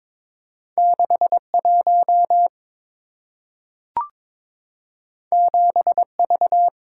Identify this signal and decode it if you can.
Morse code, heard as a keyed tone, about 22 wpm, 690 Hz. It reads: 61 7V
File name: practice / 22wpm / Two Letter-Number Combinations - 22wpm